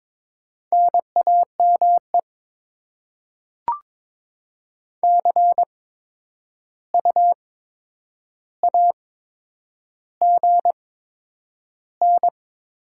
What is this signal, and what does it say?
Morse code, heard as a keyed tone, about 22 wpm, spaced as for 9 wpm, 700 Hz. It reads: NAME CU AGN
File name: Top 20 QSO Elements - 8x Character Spacing - 22wpm